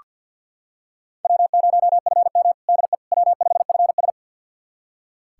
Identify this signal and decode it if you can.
Morse code, heard as a keyed tone, about 46 wpm, 700 Hz. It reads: W0FM DE W5FS
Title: USA Callsign DE Callsign - 4 Characters - 50wpm